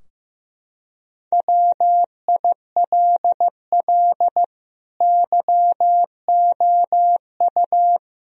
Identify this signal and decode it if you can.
Morse code, heard as a keyed tone, about 15 wpm, 700 Hz. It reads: WILL YOU
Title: Sentences from Top 1000 Words - Review Part 4 - 15wpm